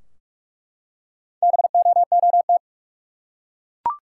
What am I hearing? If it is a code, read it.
Morse code, heard as a keyed tone, about 46 wpm, 700 Hz. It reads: BOOT
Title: Four Letter Words - 45wpm